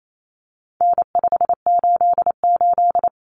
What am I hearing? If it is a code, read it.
Morse code, heard as a keyed tone, about 28 wpm, 700 Hz. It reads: N588
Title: Common Cut Numbers - 4-Digit 28wpm